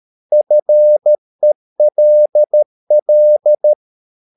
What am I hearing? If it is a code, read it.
Morse code, heard as a keyed tone, about 13 wpm, 600 Hz. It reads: FELL
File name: mp3 files / Lexiphiles - 13wpm